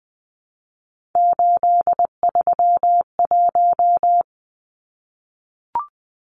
Morse code, heard as a keyed tone, about 20 wpm, 700 Hz. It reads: 831
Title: Numbers - 3-Digit 20wpm